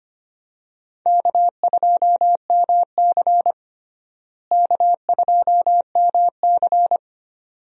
Morse code, heard as a keyed tone, about 25 wpm, 700 Hz. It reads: K2MC K2MC
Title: USA Call Signs - 4 Characters - Repeated 2x - 25wpm